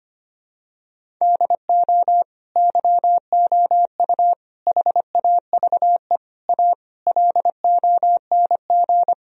Morse code, heard as a keyed tone, about 25 wpm, 700 Hz. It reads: DO YOU HAVE A LONG
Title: Sentences from Top 1000 Words - Review Part 4 - 25wpm